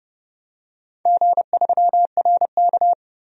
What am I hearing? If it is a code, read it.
Morse code, heard as a keyed tone, about 30 wpm, 700 Hz. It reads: G3RK
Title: DXCC-100 Call Signs - 4 Characters - 30wpm